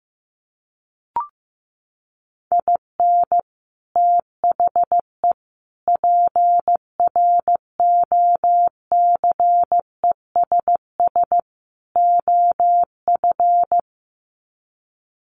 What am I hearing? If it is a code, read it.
Morse code, heard as a keyed tone, about 15 wpm, 700 Hz. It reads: IN THE PROCESS OF
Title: Sets of 4 Words 15wpm